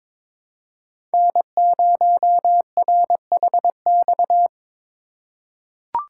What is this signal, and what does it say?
Morse code, heard as a keyed tone, about 22 wpm, 700 Hz. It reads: N0RHX
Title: USA Call Signs - 5 Characters - 2x Character Spacing - 22wpm